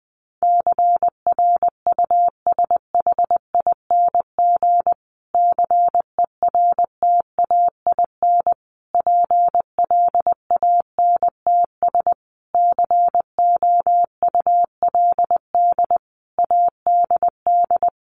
Morse code, heard as a keyed tone, about 20 wpm, 700 Hz. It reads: CRUSHING CERTAIN PLANTS COULD ADD
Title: A Sound Of Thunder 20wpm